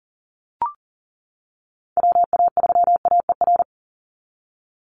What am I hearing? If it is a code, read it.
Morse code, heard as a keyed tone, about 40 wpm, 700 Hz. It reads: WA3AER
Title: Call Signs - 40wpm